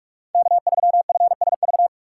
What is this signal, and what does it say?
Morse code, heard as a keyed tone, about 46 wpm, 700 Hz. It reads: K3FSV
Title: USA Call Signs - 5 Characters - 2x Character Spacing - 45wpm